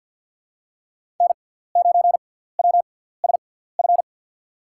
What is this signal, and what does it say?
Morse code, heard as a keyed tone, about 52 wpm, spaced as for 17 wpm, 700 Hz. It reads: N9WSF
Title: USA Call Signs - 5 Characters - 6x Character Spacing - 50wpm